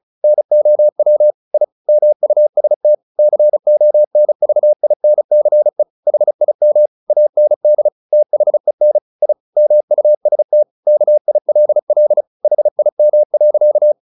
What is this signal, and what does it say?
Morse code, heard as a keyed tone, about 34 wpm, 600 Hz. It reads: NOW I MUST CONVINCE HIM AND THEN I MUST KILL HIM.
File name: the old man and the sea 2 35wpm